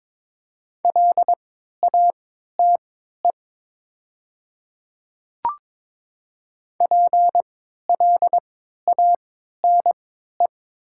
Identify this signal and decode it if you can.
Morse code, heard as a keyed tone, about 22 wpm, spaced as for 12 wpm, 700 Hz. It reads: LATE PLANE
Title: 3x Character Spacing - Top 500 Words - 22wpm